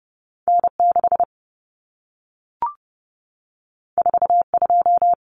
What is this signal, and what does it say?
Morse code, heard as a keyed tone, about 30 wpm, 700 Hz. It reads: N6 42
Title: Common Cut Numbers - 2-Digit 30wpm